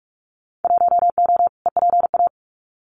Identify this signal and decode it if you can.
Morse code, heard as a keyed tone, about 46 wpm, 700 Hz. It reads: 1O EPA
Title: Winter Field Day Exchanges 45wpm